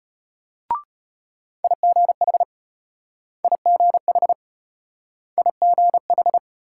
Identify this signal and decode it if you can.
Morse code, heard as a keyed tone, about 34 wpm, 700 Hz. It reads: IGH IGH IGH